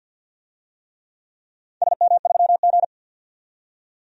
Morse code, heard as a keyed tone, about 50 wpm, 700 Hz. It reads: SM3G
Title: DXCC-100 Call Signs - 4 Characters - 50wpm